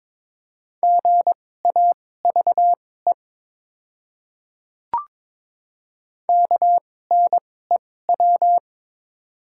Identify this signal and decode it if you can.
Morse code, heard as a keyed tone, about 22 wpm, spaced as for 16 wpm, 690 Hz. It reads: GAVE KNEW
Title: 2x Character Spacing - Top 500 Words - 22wpm